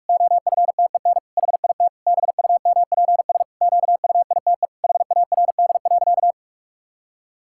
Morse code, heard as a keyed tone, about 46 wpm, 700 Hz. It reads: OFTEN HIT BUMPS QUITE HARD.
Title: Pangram Challenge v2 - 45wpm